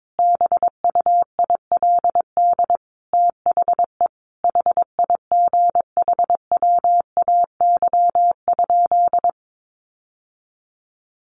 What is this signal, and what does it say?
Morse code, heard as a keyed tone, about 22 wpm, 700 Hz. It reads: BUILD THE HIGHWAY?
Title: Sentences from Top 1400 Words 22wpm